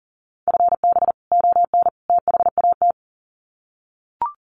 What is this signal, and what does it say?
Morse code, heard as a keyed tone, about 40 wpm, 700 Hz. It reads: FB ON THAT
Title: Common QSO Phrases 40wpm